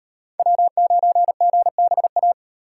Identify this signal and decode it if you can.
Morse code, heard as a keyed tone, about 38 wpm, 700 Hz. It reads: W9GBA